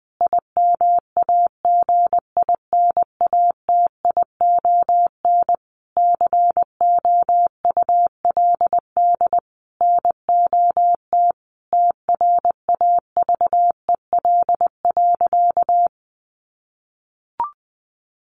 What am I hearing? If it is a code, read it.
Morse code, heard as a keyed tone, about 20 wpm, 700 Hz. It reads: IMAGINATION COULD NOT TRAVEL.
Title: Sentences from Top 1300 Words 20wpm